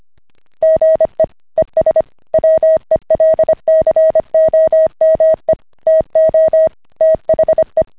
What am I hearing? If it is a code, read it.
Morse code, heard as a keyed tone, about 25 wpm, 640 Hz. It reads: GE ES WELCOME TO THE